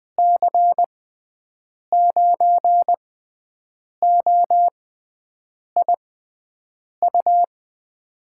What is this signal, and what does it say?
Morse code, heard as a keyed tone, about 20 wpm, spaced as for 7 wpm, 700 Hz. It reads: C9OIU
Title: USA Call Signs - 6x Character Spacing - 20wpm